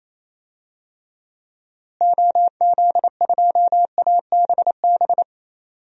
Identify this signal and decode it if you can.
Morse code, heard as a keyed tone, about 28 wpm, 700 Hz. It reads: OZ2ABB
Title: DXCC-100 Call Signs - 6 Characters - Repeated 3x - 28wpm